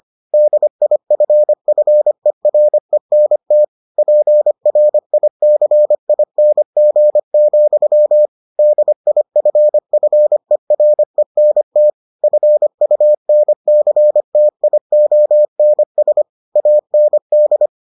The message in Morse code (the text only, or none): DIFFERENT PRICING, DIFFERENT FUNCTIONS AND